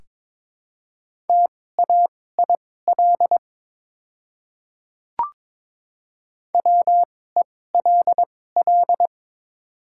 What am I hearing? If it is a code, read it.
Morse code, heard as a keyed tone, about 21 wpm, spaced as for 16 wpm, 700 Hz. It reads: TAIL WELL
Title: 2x Character Spacing - Top 500 Words - 22wpm